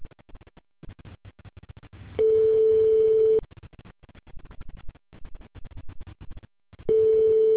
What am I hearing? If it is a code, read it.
Ringback tone (France), 440 Hz.